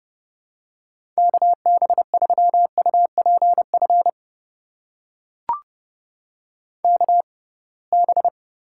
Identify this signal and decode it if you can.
Morse code, heard as a keyed tone, about 32 wpm, spaced as for 23 wpm, 700 Hz. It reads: KB3UPF KB
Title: USA Call Signs - 6 Characters - 6x Character Spacing - 30wpm